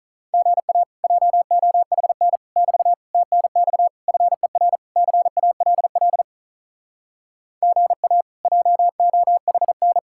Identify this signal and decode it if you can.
Morse code, heard as a keyed tone, about 45 wpm, 700 Hz. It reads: GA JOHN <BT> TNX FER CALL GA JOHN